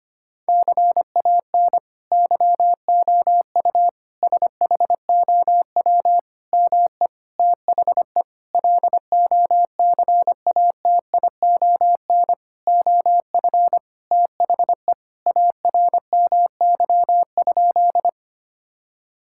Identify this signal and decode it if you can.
Morse code, heard as a keyed tone, about 25 wpm, 700 Hz. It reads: CAN YOU SHOW ME THE LOCATION OF THE ARMY?
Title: Sentences from Top 1000 Words - Review Part 6 - 25wpm